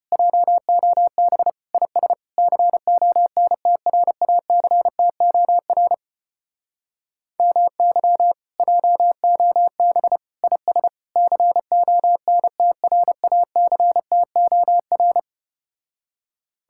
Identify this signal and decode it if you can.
Morse code, heard as a keyed tone, about 33 wpm, 700 Hz. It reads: JOB IS CONTRACTOR MY JOB IS CONTRACTOR